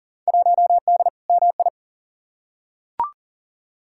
Morse code, heard as a keyed tone, about 37 wpm, 700 Hz. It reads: 1D MI
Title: Field Day Exchanges - 40wpm